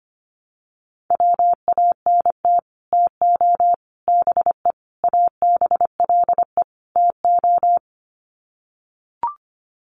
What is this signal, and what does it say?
Morse code, heard as a keyed tone, about 25 wpm, 700 Hz. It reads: WANT TO BE ABLE TO
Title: Sets of 5 Words 25wpm